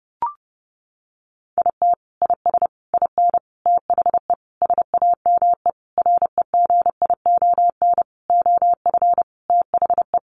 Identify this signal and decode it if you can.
Morse code, heard as a keyed tone, about 30 wpm, 700 Hz. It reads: IT IS IN THE SAME REGION OF THE
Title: Sentences from Top 1500 Words 30wpm